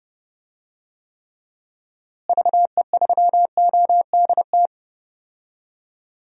Morse code, heard as a keyed tone, about 30 wpm, 700 Hz. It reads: VE3ODT
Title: DXCC-100 Call Signs - 6 Characters - Repeated 3x - 30wpm